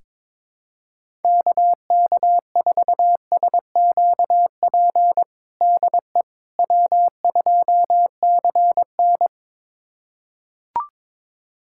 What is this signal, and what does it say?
Morse code, heard as a keyed tone, about 22 wpm, 700 Hz. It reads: KK4SQP DE W2CN
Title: USA Callsign DE Callsign 22wpm